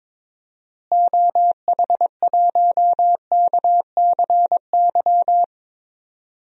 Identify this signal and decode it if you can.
Morse code, heard as a keyed tone, about 22 wpm, 700 Hz. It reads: OH1KCY
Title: DXCC-100 Call Signs - 6 Characters - Repeated 3x - 22wpm